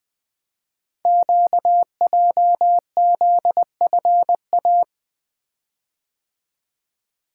Morse code, heard as a keyed tone, about 20 wpm, 700 Hz. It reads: QJZFA
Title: Code Groups - 5 Letters - 20wpm